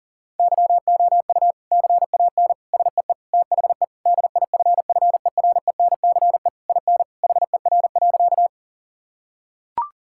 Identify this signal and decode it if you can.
Morse code, heard as a keyed tone, about 40 wpm, 700 Hz. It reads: YOU CAN SEE THE DIFFERENCE IN HER.
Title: Sentences from Top 1000 Words - Review Part 5 - 40wpm